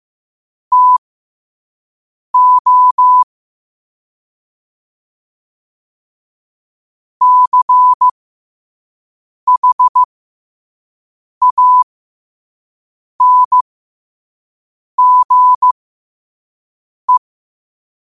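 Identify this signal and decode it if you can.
Morse code, heard as a keyed tone, about 15 wpm, spaced as for 5 wpm, 1000 Hz. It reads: TO CHANGE
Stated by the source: Morse quote of the day at 5 wpm.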